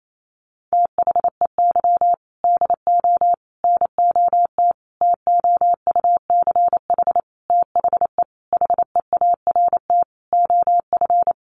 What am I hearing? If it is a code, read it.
Morse code, heard as a keyed tone, about 28 wpm, 700 Hz. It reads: THEY DO NOT TOUCH THE HEART OF
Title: Sentences from Top 1200 Words 28wpm